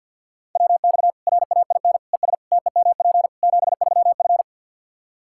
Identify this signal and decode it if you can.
Morse code, heard as a keyed tone, about 52 wpm, 700 Hz. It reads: WX RAIN ES TEMP 73F
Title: Practice QSO - Exchange 2 - Info - 50wpm